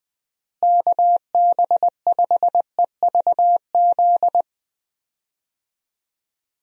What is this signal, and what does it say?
Morse code, heard as a keyed tone, about 21 wpm, 700 Hz. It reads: KB5EVZ
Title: USA Call Signs - 6 Characters - Repeated 2x - 20wpm